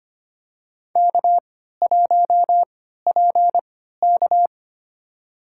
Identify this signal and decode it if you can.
Morse code, heard as a keyed tone, about 25 wpm, spaced as for 14 wpm, 700 Hz. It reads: K1PK
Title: USA Call Signs - 4 Characters - 3x Character Spacing - 25wpm